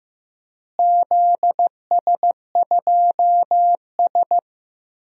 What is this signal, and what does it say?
Morse code, heard as a keyed tone, about 15 wpm, 700 Hz. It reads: ZS2S
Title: DXCC-100 Call Signs - 4 Characters - Repeated 2x - 15wpm